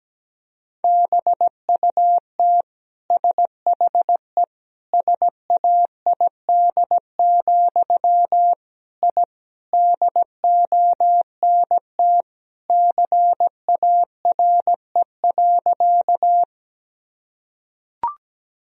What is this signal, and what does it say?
Morse code, heard as a keyed tone, about 17 wpm, 700 Hz. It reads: BUT SHE SAID, I DONT CARE.
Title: Sentences from Top 1000 Words - Review Part 1 - 17wpm